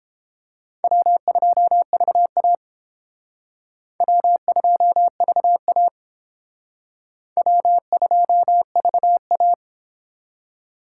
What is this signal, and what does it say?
Morse code, heard as a keyed tone, about 31 wpm, 700 Hz. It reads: W2VA W2VA W2VA